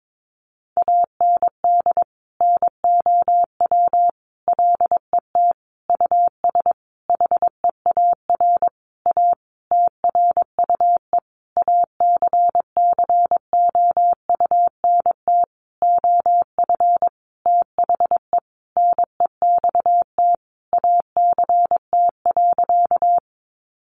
Morse code, heard as a keyed tone, about 22 wpm, 700 Hz. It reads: AND NOW LET US HEAR A TRUE ACCOUNT OF THE NEXT ACT.
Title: Sentences from Top 900 Words 22wpm